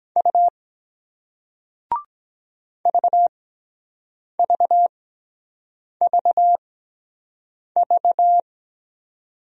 Morse code, heard as a keyed tone, about 23 wpm, spaced as for 7 wpm, 700 Hz. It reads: U VVVV